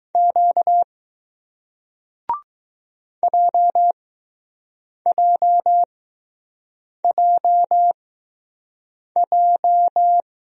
Morse code, heard as a keyed tone, about 20 wpm, spaced as for 6 wpm, 700 Hz. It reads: Q JJJJ